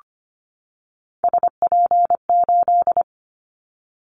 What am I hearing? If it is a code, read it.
Morse code, heard as a keyed tone, about 25 wpm, 700 Hz. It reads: SP8
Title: Callsign Prefixes for Top 25 DXCC - 25wpm